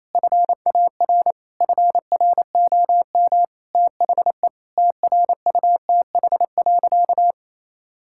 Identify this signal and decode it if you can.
Morse code, heard as a keyed tone, about 28 wpm, 700 Hz. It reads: FAR FROM THE TRUTH.
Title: Sentences from Top 1600 Words 28wpm